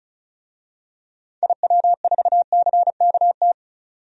Morse code, heard as a keyed tone, about 34 wpm, 700 Hz. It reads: IW4CKT
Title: DXCC-100 Call Signs - 6 Characters - 35wpm